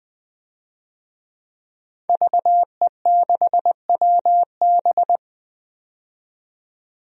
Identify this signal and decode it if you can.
Morse code, heard as a keyed tone, about 20 wpm, 700 Hz. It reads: VE6WB
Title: DXCC-100 Call Signs - 5 Characters - 20wpm